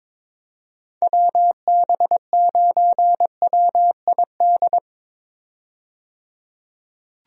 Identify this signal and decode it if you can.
Morse code, heard as a keyed tone, about 22 wpm, 700 Hz. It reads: WB9WID
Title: USA Call Signs - 6 Characters - 22wpm